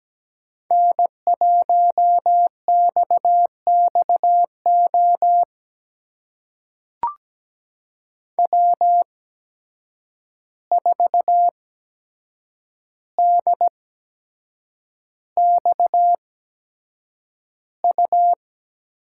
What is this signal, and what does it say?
Morse code, heard as a keyed tone, about 17 wpm, spaced as for 7 wpm, 700 Hz. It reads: N1XXO W4DXU
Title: USA Call Signs - 5 Characters - 8x Character Spacing - 17wpm